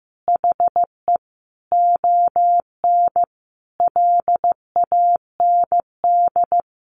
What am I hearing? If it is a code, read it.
Morse code, heard as a keyed tone, about 14 wpm, 700 Hz. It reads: HE ON LAND
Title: Sentences from Top 500 Words 15wpm